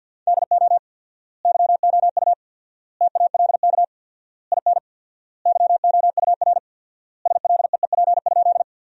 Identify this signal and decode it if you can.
Morse code, heard as a keyed tone, about 51 wpm, 700 Hz. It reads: DO YOU TALK IN YOUR SLEEP?
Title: 4x Word-Spacing - Sentences from Top 500 Words - 50wpm